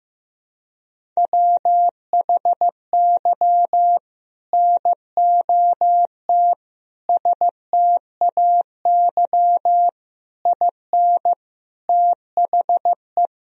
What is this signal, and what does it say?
Morse code, heard as a keyed tone, about 15 wpm, 700 Hz. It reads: WHY NOT STAY IN THE
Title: Sentences from Top 1000 Words - Review Part 1 - 15wpm